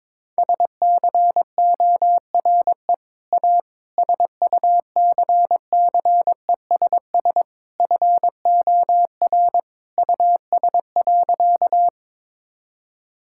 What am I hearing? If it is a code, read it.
Morse code, heard as a keyed tone, about 22 wpm, 700 Hz. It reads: SCORE A SUCCESS FOR US.
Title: Sentences from Top 1600 Words 22wpm